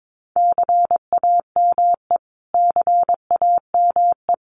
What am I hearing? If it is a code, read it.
Morse code, heard as a keyed tone, about 22 wpm, 700 Hz. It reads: CAME CAME
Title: Top 200 Words Repeated 22wpm